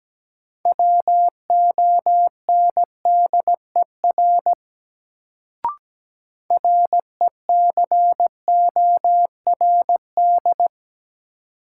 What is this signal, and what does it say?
Morse code, heard as a keyed tone, about 17 wpm, 700 Hz. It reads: WONDER RECORD
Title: Sentences from Top 500 Words - New Word Drill - 17wpm